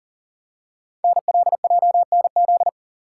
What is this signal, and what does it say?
Morse code, heard as a keyed tone, about 42 wpm, 700 Hz. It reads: NLJNZ
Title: Code Groups - 5 Letters - 40wpm